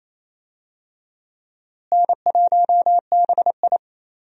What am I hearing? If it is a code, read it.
Morse code, heard as a keyed tone, about 28 wpm, 700 Hz. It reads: N1BI
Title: USA Call Signs - 4 Characters - Repeated 2x - 28wpm